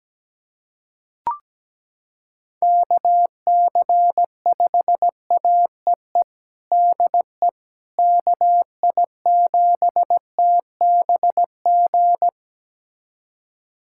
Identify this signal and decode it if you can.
Morse code, heard as a keyed tone, about 17 wpm, 700 Hz. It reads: KC5AEE DE KI7TBG
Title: USA Callsign DE Callsign - 6 Characters - 17wpm